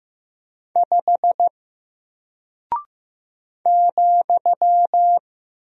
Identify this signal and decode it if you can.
Morse code, heard as a keyed tone, about 15 wpm, 700 Hz. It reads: 5 ,
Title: Single Character (Letter:Number:Punctuation:Prosign) Alt 15wpm